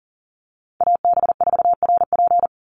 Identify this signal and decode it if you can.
Morse code, heard as a keyed tone, about 38 wpm, 690 Hz. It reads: AB4RP
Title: USA Call Signs 40wpm